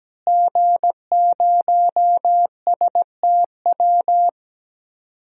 Morse code, heard as a keyed tone, about 17 wpm, 700 Hz. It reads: G0STW
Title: DXCC-100 Call Signs - 5 Characters - Repeated 3x - 17wpm